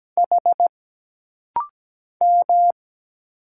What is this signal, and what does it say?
Morse code, heard as a keyed tone, about 17 wpm, 700 Hz. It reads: H M
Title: Single Letter-Number Alt - Rapid-Fire 17wpm